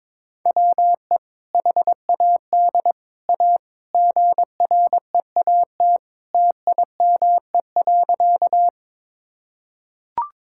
Morse code, heard as a keyed tone, about 22 wpm, 700 Hz. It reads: WE HAD A GREAT TIME.
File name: practice / 22wpm / Sentences with 2-Word Phrases - 22wpm